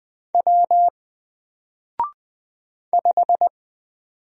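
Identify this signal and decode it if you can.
Morse code, heard as a keyed tone, about 20 wpm, 700 Hz. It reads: W 5
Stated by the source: Single letter and number Morse drill at 20 wpm, from N4LCD.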